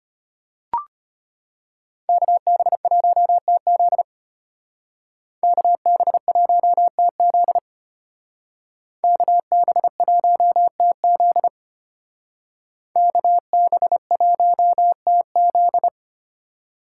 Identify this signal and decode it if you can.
Morse code, heard as a keyed tone, about 32 wpm, 700 Hz. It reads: KB1TZ KB1TZ KB1TZ KB1TZ